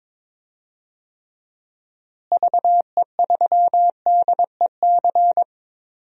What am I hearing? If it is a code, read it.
Morse code, heard as a keyed tone, about 23 wpm, 700 Hz. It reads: VE3DEC
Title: DXCC-100 Call Signs - 6 Characters - Repeated 3x - 22wpm